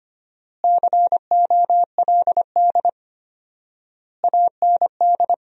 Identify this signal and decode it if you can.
Morse code, heard as a keyed tone, about 25 wpm, 700 Hz. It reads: COLD AND
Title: 4x Word-Spacing - Sentences from Top 400 Words - 25wpm